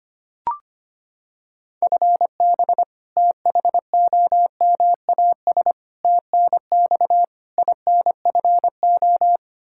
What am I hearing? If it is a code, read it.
Morse code, heard as a keyed tone, about 25 wpm, 700 Hz. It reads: FB THOMAS TNX INFO
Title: Practice QSO - Exchange 3 - More Info - 25wpm